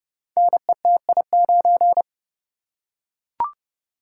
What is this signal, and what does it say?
Morse code, heard as a keyed tone, about 30 wpm, 700 Hz. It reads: NETI9
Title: Code Groups - 5 Letter:Numbers - 30wpm